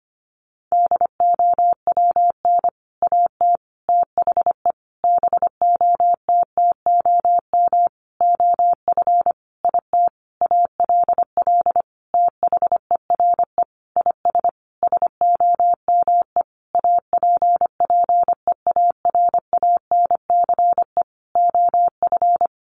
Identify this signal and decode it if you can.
Morse code, heard as a keyed tone, about 25 wpm, 700 Hz. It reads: DOWN AT THE BOTTOM OF IT ALL THERE IS SOME APPEARANCE OF
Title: Sentences from Top 1100 Words 25wpm